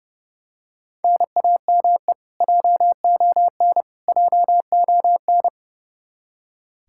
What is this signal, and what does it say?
Morse code, heard as a keyed tone, about 32 wpm, 700 Hz. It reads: NAME JON JON
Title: K1USN SST QSO Practice 30wpm